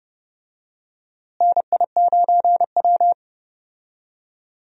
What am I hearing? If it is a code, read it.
Morse code, heard as a keyed tone, about 30 wpm, 700 Hz. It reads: NI9W
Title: USA Call Signs - 4 Characters - Repeated 3x - 30wpm